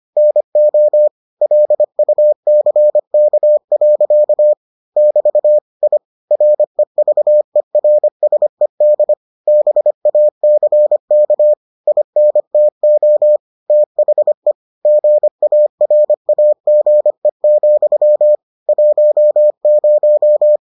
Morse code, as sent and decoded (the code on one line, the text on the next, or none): -. --- / .-.. ..- -.-. -.- .-.-.- / -...- / .. / .-. ...- . .-. ... -.. / -... .- -.-. -.- / .. -. - --- / - .... / --. .- .-. .- --. --..-- / .---- -----
NO LUCK. <BT> I REVERSED BACK INTO THE GARAGE, 10